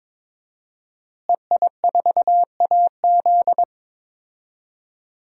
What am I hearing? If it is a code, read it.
Morse code, heard as a keyed tone, about 22 wpm, 700 Hz. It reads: EI4AZ
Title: DXCC-100 Call Signs - 5 Characters - Repeated 2x - 22wpm